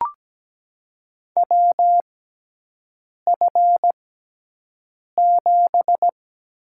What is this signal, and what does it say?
Morse code, heard as a keyed tone, about 17 wpm, spaced as for 6 wpm, 700 Hz. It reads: WF7
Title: USA Call Signs - 4 Characters - 6x Character Spacing - 17wpm